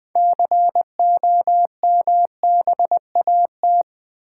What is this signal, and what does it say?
Morse code, heard as a keyed tone, about 20 wpm, 700 Hz. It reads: COMBAT
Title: Build A Word - 2 Letters At A Time - 6-Letter Words - 20wpm